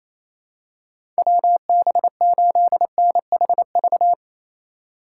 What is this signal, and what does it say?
Morse code, heard as a keyed tone, about 27 wpm, 700 Hz. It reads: WB8NHV
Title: USA Call Signs - 6 Characters - 8x Character Spacing - 28wpm